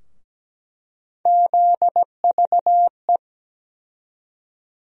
Morse code, heard as a keyed tone, about 17 wpm, 700 Hz. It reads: ZVE
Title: USA Call Sign Postfixes - 3 Letters 17wpm